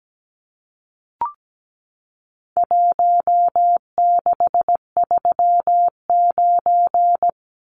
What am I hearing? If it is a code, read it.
Morse code, heard as a keyed tone, about 17 wpm, 700 Hz. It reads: 1639
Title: Numbers - Contest Style - 4-Digit 17wpm